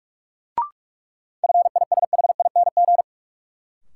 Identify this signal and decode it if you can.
Morse code, heard as a keyed tone, about 46 wpm, 700 Hz. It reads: FISHING
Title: Sentences from Top 1000 Words - Review Part 5 - New Word Drill - 45wpm